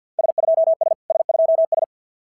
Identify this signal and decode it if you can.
Morse code, heard as a keyed tone, about 50 wpm, 650 Hz. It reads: S2S S2S
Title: SOTA QSO Practice 50wpm